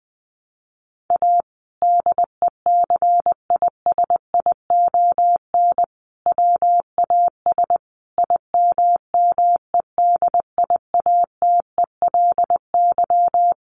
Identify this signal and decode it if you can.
Morse code, heard as a keyed tone, about 20 wpm, 700 Hz. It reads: A DECISION WAS IMMEDIATELY
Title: Sentences from Top 1100 Words 20wpm